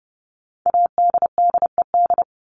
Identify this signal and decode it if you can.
Morse code, heard as a keyed tone, about 30 wpm, 700 Hz. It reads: ADDED
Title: ED Suffix - Top 100 Words with Base Words 30wpm